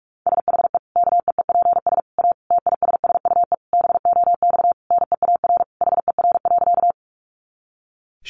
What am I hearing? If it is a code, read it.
Morse code, heard as a keyed tone, about 44 wpm, 700 Hz. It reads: SHE KEEPS A TISSUE BOX NEAR HER.
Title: Easy Phrases 45wpm